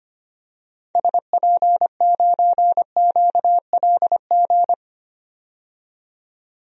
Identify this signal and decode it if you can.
Morse code, heard as a keyed tone, about 25 wpm, 700 Hz. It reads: SP9QLG
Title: DXCC-100 Call Signs - 6 Characters - 25wpm